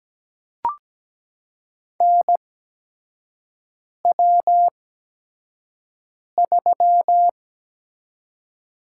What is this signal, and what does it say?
Morse code, heard as a keyed tone, about 17 wpm, spaced as for 5 wpm, 700 Hz. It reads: NW3
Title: USA Call Signs - 4 Characters - 8x Character Spacing - 17wpm